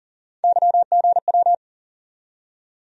Morse code, heard as a keyed tone, about 40 wpm, 700 Hz. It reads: YGW
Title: USA Call Sign Postfixes - 3 Letters 40wpm